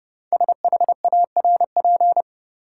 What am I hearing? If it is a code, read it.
Morse code, heard as a keyed tone, about 32 wpm, 700 Hz. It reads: SHARP
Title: Sentences from Top 1600 Words - New Word Drill Alt - 30wpm